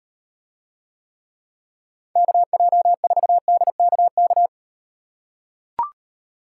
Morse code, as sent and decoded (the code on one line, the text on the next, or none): -.- .--- ....- -.. -.- -.-
KJ4DKK